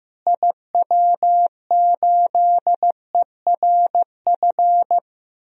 Morse code, heard as a keyed tone, about 15 wpm, 700 Hz. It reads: IW8ERF
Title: DXCC-100 Call Signs - 6 Characters - Repeated 3x - 15wpm